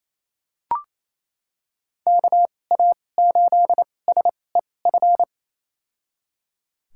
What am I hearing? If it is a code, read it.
Morse code, heard as a keyed tone, about 28 wpm, spaced as for 20 wpm, 700 Hz. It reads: KA8SEF
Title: USA Call Signs - 6 Characters - 2x Character Spacing - 28wpm